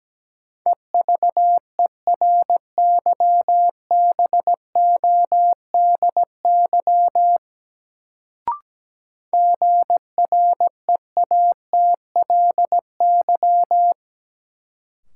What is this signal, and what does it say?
Morse code, heard as a keyed tone, about 17 wpm, 700 Hz. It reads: EVERYBODY GREATLY
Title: Sentences from Top 1400 Words - New Word Drill - 17wpm